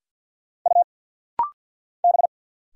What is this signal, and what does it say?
Morse code, heard as a keyed tone, about 49 wpm, 700 Hz. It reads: U B
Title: Single Letter-Number Alt - Rapid-Fire 50wpm